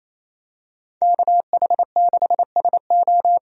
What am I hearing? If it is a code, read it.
Morse code, heard as a keyed tone, about 28 wpm, 700 Hz. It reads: KH6SO
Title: DXCC-100 Call Signs - 5 Characters - 28wpm